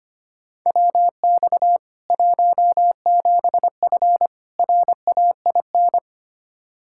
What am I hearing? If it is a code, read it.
Morse code, heard as a keyed tone, about 25 wpm, 700 Hz. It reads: WX 17F RAIN
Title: K1USN SST QSO Practice 25wpm